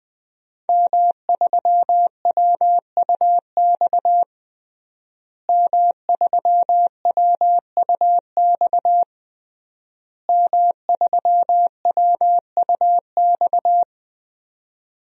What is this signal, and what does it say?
Morse code, heard as a keyed tone, about 20 wpm, 700 Hz. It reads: M3WUX M3WUX M3WUX
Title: DXCC-100 Call Signs - 5 Characters - Repeated 3x - 20wpm